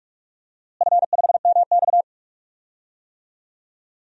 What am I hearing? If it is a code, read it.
Morse code, heard as a keyed tone, about 44 wpm, 700 Hz. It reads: F5MX
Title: DXCC-100 Call Signs - 4 Characters - Repeated 3x - 45wpm